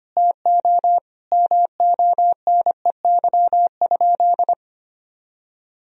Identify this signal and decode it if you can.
Morse code, heard as a keyed tone, about 25 wpm, 700 Hz. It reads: TO MONEY?
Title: Sentences from Top 1000 Words - Review Part 4 - 25wpm